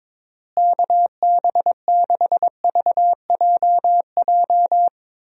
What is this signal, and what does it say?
Morse code, heard as a keyed tone, about 22 wpm, 700 Hz. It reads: KB6VJJ
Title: USA Call Signs - 6 Characters - Repeated 3x - 22wpm